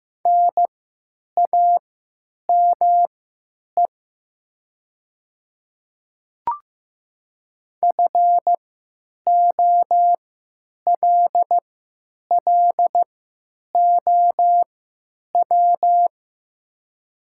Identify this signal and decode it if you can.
Morse code, heard as a keyed tone, about 15 wpm, spaced as for 9 wpm, 700 Hz. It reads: NAME FOLLOW
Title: 3x Character Spacing - Top 200 Words - 15wpm